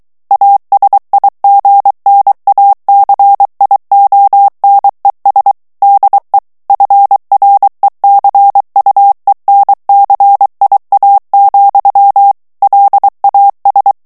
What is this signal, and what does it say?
Morse code, heard as a keyed tone, about 23 wpm, 800 Hz. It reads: ASIGNACIONES DE FRECUENCIA, LAS
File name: morse mp3 / ESPAGNOL TEXTE EN CLAIR 2 VITESSE 22 MOTS MINUTES